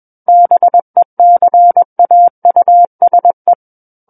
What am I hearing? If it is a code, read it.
Morse code, heard as a keyed tone, about 21 wpm, 700 Hz. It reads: BECAUSE
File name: Book One / 12-Chapter 12 - 21 WPM